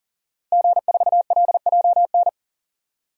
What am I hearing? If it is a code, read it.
Morse code, heard as a keyed tone, about 40 wpm, 700 Hz. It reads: G4LJN
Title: DXCC-100 Call Signs - 5 Characters - Repeated 2x - 40wpm